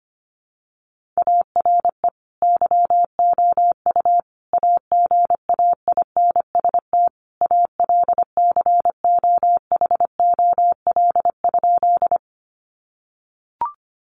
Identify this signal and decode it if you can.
Morse code, heard as a keyed tone, about 25 wpm, 700 Hz. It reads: ARE YOU AGAINST ALCOHOL?
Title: Sentences from Top 1400 Words 25wpm